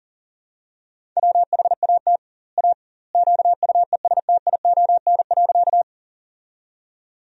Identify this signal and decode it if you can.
Morse code, heard as a keyed tone, about 40 wpm, 700 Hz. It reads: WHAT A QUESTION.
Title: 2x Word-Spacing - Sentences from Top 400 Words - 40wpm